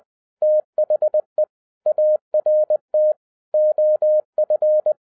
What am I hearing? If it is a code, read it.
Morse code, heard as a keyed tone, about 20 wpm, 610 Hz. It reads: THE ART OF
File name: 2011-07-mp3 / n08 A LOT HAS CHANGED 20wpm